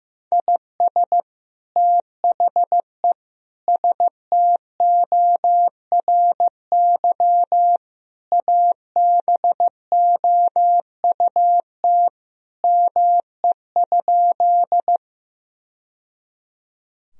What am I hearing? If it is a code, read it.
Morse code, heard as a keyed tone, about 15 wpm, 700 Hz. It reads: IS THE STORY ABOUT ME?
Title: Sentences from Top 1000 Words - Review Part 1 - 15wpm